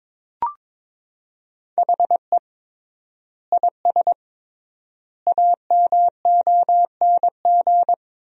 Morse code, heard as a keyed tone, about 21 wpm, 700 Hz. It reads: HE IS AMONG
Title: 3x Word-Spacing - Sentences from Top 500 Words - 22wpm